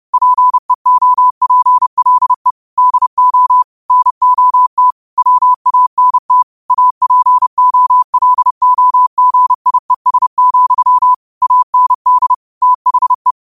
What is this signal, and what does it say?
Morse code, heard as a keyed tone, about 30 wpm, 1000 Hz. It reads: PEOPLE DO NOT WANT APOLOGIES, AND THE
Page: New quotes every day in morse code at 30 Words per minute.